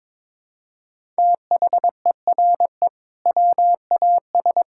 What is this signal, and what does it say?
Morse code, heard as a keyed tone, about 22 wpm, 700 Hz. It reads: THERE WAS
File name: practice / 22wpm / Build A Phrase - 5-Word Phrases - 22wpm